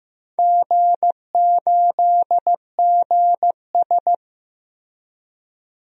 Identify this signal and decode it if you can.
Morse code, heard as a keyed tone, about 15 wpm, 700 Hz. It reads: G8GS
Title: DXCC-100 Call Signs - 4 Characters - Repeated 2x - 15wpm